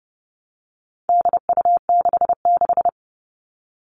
Morse code, heard as a keyed tone, about 30 wpm, 700 Hz. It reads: DU66
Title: All Cut Numbers - 4-Digit 30wpm